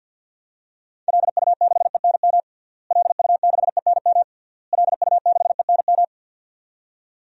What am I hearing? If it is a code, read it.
Morse code, heard as a keyed tone, about 50 wpm, 700 Hz. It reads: LU6ENM LU6ENM LU6ENM
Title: DXCC-100 Call Signs - 6 Characters - Repeated 3x - 50wpm